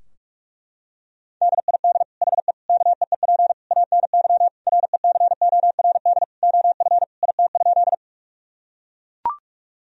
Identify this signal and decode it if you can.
Morse code, heard as a keyed tone, about 41 wpm, 700 Hz. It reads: DID HE KEEP ANY RECORD OF IT?
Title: Sentences from Top 1000 Words - Review Part 3 - 45wpm